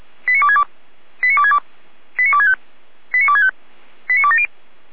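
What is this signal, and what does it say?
Five-tone selective call (ZVEI), heard as quick runs of five tones, tones of about 70 ms. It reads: 89262 89262 89265 89265 89270
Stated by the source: ZVEI five-tone selective call received by SDR.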